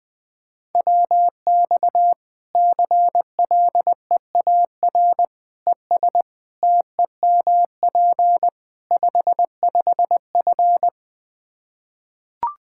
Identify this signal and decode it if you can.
Morse code, heard as a keyed tone, about 21 wpm, 700 Hz. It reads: WX CLEAR ES TEMP 55F
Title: Practice QSO - Exchange 2 - Info - 20wpm